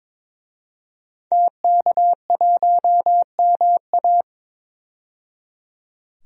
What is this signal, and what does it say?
Morse code, heard as a keyed tone, about 22 wpm, 700 Hz. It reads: TK1MA
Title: DXCC-100 Call Signs - 5 Characters - 22wpm